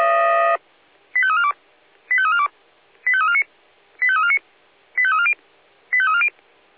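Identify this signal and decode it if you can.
Five-tone selective call (ZVEI), heard as quick runs of five tones, tones of about 70 ms. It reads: 85342 85342 85349 85349 85340 85340
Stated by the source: ZVEI five-tone selective call 85279, received with an SDR.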